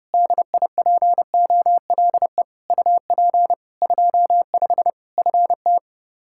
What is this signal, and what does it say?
Morse code, heard as a keyed tone, about 29 wpm, 700 Hz. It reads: DIPOLE UP 25 FT
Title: K1USN SST QSO Practice 30wpm